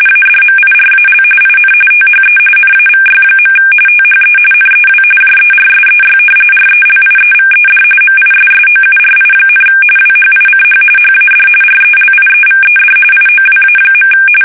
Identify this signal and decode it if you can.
RTTY, about 45.45 baud. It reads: ADMIRAL IN THE NAVY. THE NEW JERSEY RELIEVED MISSOURI, WHICH HAD RELIEVED IOWA.
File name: fsk850 / battleshipnj